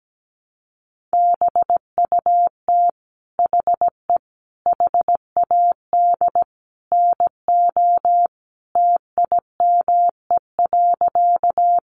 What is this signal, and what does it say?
Morse code, heard as a keyed tone, about 17 wpm, 700 Hz. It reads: BUT HE HAD NO TIME.
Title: Sentences from Top 100 Words 17wpm